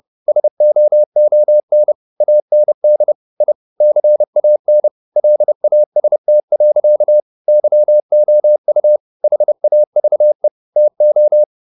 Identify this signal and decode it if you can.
Morse code, heard as a keyed tone, about 30 wpm, 600 Hz. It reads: SOON AND I CAN LAST. YOU HAVE TO